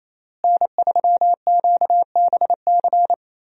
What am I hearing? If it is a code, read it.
Morse code, heard as a keyed tone, about 28 wpm, 700 Hz. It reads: N3QBC
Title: USA Call Signs - 5 Characters - Repeated 2x - 28wpm